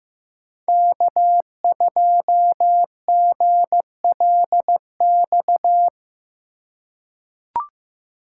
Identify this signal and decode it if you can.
Morse code, heard as a keyed tone, about 15 wpm, 700 Hz. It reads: K2GLX
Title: USA Call Signs - 5 Characters - 4x Character Spacing - 15wpm